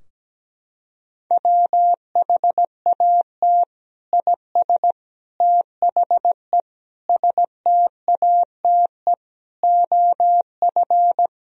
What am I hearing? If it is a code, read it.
Morse code, heard as a keyed tone, about 17 wpm, 700 Hz. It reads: WHAT IS THE STATE OF
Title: Sentences from Top 1000 Words - Review Part 3 - 17wpm